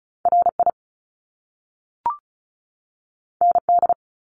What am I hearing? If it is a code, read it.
Morse code, heard as a keyed tone, about 36 wpm, 700 Hz. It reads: RI ND
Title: Top Two Letter Combinations in Words 35wpm